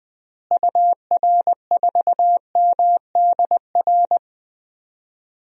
Morse code, heard as a keyed tone, about 20 wpm, 700 Hz. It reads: UR4MDR
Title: DXCC-100 Call Signs - 6 Characters - Repeated 3x - 20wpm